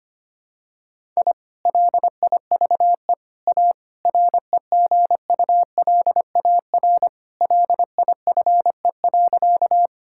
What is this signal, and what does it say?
Morse code, heard as a keyed tone, about 25 wpm, 700 Hz. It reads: I LIVE A REGULAR LIFE.
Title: Sentences from Top 1000 Words - Review Part 6 - 25wpm